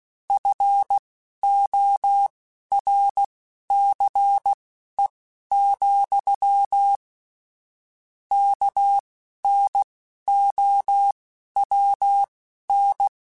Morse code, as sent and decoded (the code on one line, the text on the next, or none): ..-. --- .-. -.-. --..-- / -.- -. --- .-- -.
FORCE, KNOWN